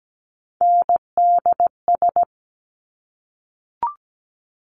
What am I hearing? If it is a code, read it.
Morse code, heard as a keyed tone, about 17 wpm, 690 Hz. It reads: NDS
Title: Top Three Letter Combinations in Words 17wpm